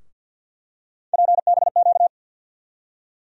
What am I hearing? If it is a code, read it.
Morse code, heard as a keyed tone, about 53 wpm, 700 Hz. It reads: PBQ
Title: USA Call Sign Postfixes - 3 Letters 50wpm